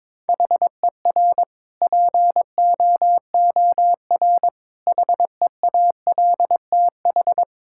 Morse code, heard as a keyed tone, about 22 wpm, 700 Hz. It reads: HER POOR HEALTH
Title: Sentences from Top 1000 Words - Review Part 6 - 22wpm